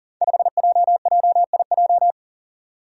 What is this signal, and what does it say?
Morse code, heard as a keyed tone, about 38 wpm, 700 Hz. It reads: 5JJIJ
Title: Five Challenging Letter-Number Combinations 40wpm